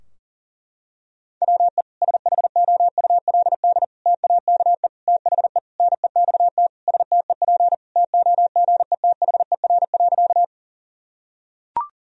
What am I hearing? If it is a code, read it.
Morse code, heard as a keyed tone, about 40 wpm, 700 Hz. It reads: WE SHOULD TAKE THE NEXT STEP TOGETHER.
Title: Sentences from Top 1000 Words - Review Part 2 - 40wpm